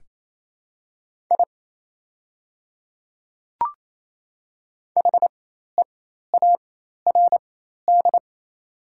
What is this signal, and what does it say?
Morse code, heard as a keyed tone, about 28 wpm, spaced as for 13 wpm, 700 Hz. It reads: I HEARD
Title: 4x Character Spacing - Top 500 Words - 28wpm